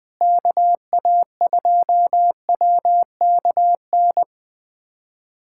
Morse code, heard as a keyed tone, about 20 wpm, 700 Hz. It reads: KA2WKN
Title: USA Callsign DE Callsign 20wpm